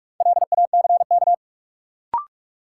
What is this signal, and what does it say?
Morse code, heard as a keyed tone, about 46 wpm, 700 Hz. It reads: LACK
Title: Sentences from Top 1100 Words - New Word Drill Alt - 45wpm